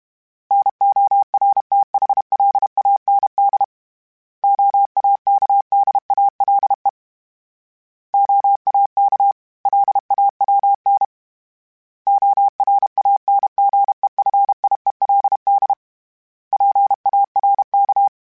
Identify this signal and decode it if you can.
Morse code, heard as a keyed tone, about 32 wpm, 800 Hz. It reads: NORTHLAND OAKDALE OAK LAWN ORANGEFIELD PARK
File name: fromSMs / FromSMs mp3s / Cities@30